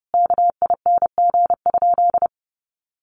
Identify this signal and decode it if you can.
Morse code, heard as a keyed tone, about 30 wpm, 700 Hz. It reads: KING?
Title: Sentences from Top 1400 Words 30wpm